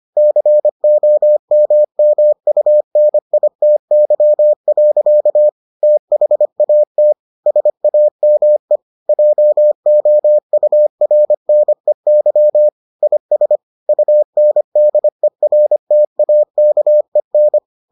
Morse code, heard as a keyed tone, about 25 wpm, 600 Hz. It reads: COMMUNITY. THAT SAME JOURNEY IS UNDERTAKEN